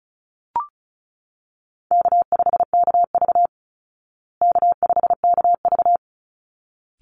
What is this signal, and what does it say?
Morse code, heard as a keyed tone, about 34 wpm, 700 Hz. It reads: K5KV K5KV
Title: Call Signs Repeated 3x - 35wpm